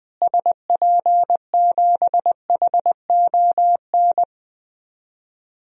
Morse code, heard as a keyed tone, about 20 wpm, 700 Hz. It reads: SP7HON
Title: DXCC-100 Call Signs - 6 Characters - Repeated 2x - 20wpm